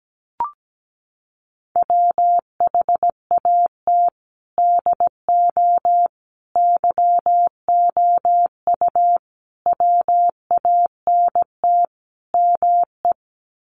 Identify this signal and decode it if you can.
Morse code, heard as a keyed tone, about 17 wpm, 700 Hz. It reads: WHAT DO YOU WANT ME
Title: Sets of 5 Words Encore2 17wpm